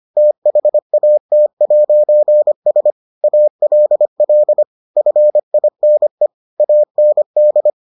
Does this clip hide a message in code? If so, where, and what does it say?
Yes: Morse code from 0.2 to 7.7 s, heard as a keyed tone, about 25 wpm, 600 Hz. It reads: THAT'S ALL FINE AND